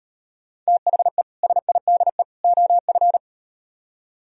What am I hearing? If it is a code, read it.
Morse code, heard as a keyed tone, about 38 wpm, 700 Hz. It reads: THE SIDE OF